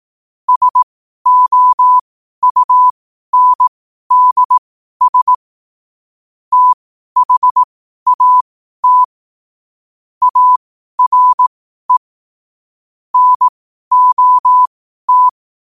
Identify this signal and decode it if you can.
Morse code, heard as a keyed tone, about 18 wpm, spaced as for 12 wpm, 1000 Hz. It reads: SOUNDS THAT ARE NOT
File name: qotdfiles / 12wpm